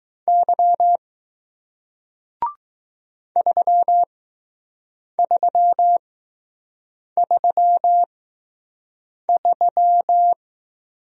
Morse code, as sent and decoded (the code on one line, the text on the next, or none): -.-- / ...-- ...-- ...-- ...--
Y 3333